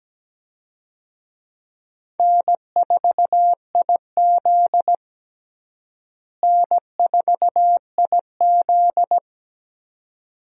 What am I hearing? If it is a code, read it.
Morse code, heard as a keyed tone, about 17 wpm, 700 Hz. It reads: N4IZ N4IZ
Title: USA Call Signs - 4 Characters - Repeated 2x - 17wpm